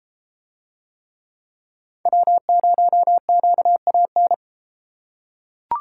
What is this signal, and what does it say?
Morse code, heard as a keyed tone, about 30 wpm, 700 Hz. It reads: W0QAN